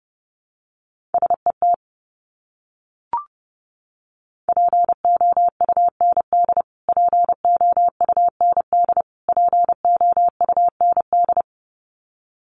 Morse code, heard as a keyed tone, about 30 wpm, 700 Hz. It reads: SET POUND POUND POUND
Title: Top 500 Words Repeated 30wpm